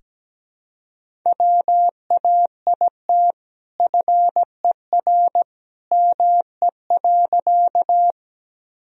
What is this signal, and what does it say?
Morse code, heard as a keyed tone, about 17 wpm, 700 Hz. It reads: WAIT FER ME.
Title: Sentences from Top 500 Words and Abbreviations 17wpm